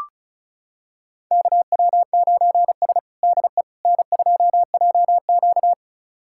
Morse code, heard as a keyed tone, about 36 wpm, 700 Hz. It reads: KW9S DE N2JQ
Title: USA Callsign DE Callsign - 4 Characters - 35wpm